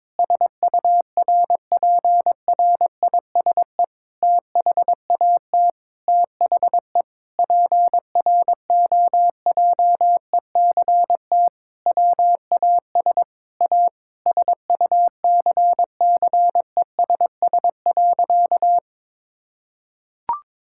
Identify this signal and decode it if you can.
Morse code, heard as a keyed tone, about 22 wpm, 700 Hz. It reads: SURPRISE THAT THE PROJECT WAS A SUCCESS.
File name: Sentences with 9-Word Phrases - 22wpm